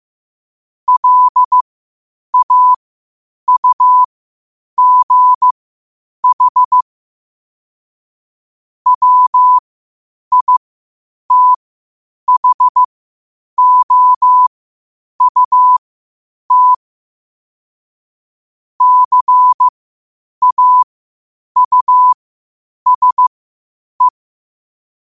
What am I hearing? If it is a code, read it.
Morse code, heard as a keyed tone, about 15 wpm, spaced as for 8 wpm, 1000 Hz. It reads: LAUGH WITHOUT CAUSE
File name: qotdfiles / 8wpm